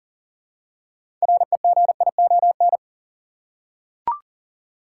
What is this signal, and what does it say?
Morse code, heard as a keyed tone, about 40 wpm, 700 Hz. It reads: REGION
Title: Six Letter Words - 40wpm